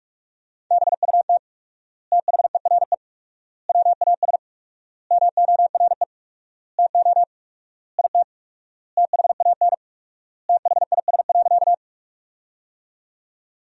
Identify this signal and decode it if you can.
Morse code, heard as a keyed tone, about 47 wpm, 700 Hz. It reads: BUT THERE WAS MORE TO IT THAN THIS.
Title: 4x Word-Spacing - Sentences from Top 100 Words - 45wpm